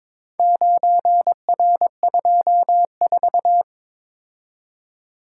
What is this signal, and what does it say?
Morse code, heard as a keyed tone, about 22 wpm, 700 Hz. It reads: 9R24
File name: practice / 22wpm / Four Letter-Number Combinations - 22wpm